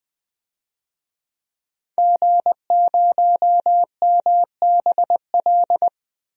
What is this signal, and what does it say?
Morse code, heard as a keyed tone, about 20 wpm, 700 Hz. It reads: G0MBL
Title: DXCC-100 Call Signs - 5 Characters - 20wpm